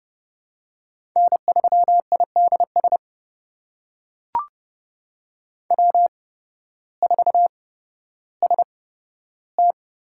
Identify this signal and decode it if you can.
Morse code, heard as a keyed tone, about 32 wpm, spaced as for 14 wpm, 700 Hz. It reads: N3IDS W4ST
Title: USA Call Signs - 5 Characters - 8x Character Spacing - 30wpm